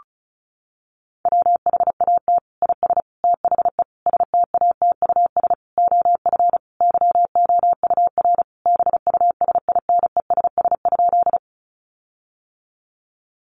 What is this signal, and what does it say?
Morse code, heard as a keyed tone, about 34 wpm, 700 Hz. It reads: WHAT IS THE STATUS OF YOUR BUSINESS?
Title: Sentences from Top 1400 Words 35wpm